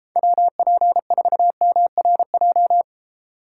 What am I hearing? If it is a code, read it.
Morse code, heard as a keyed tone, about 35 wpm, 700 Hz. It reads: WP4MRJ
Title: Speed Racing - USA 2x3 Call Signs 28wpm